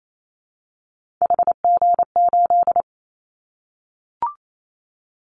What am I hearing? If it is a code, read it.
Morse code, heard as a keyed tone, about 28 wpm, 700 Hz. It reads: HG8
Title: Callsign Prefixes for Top 50 DXCC - 28wpm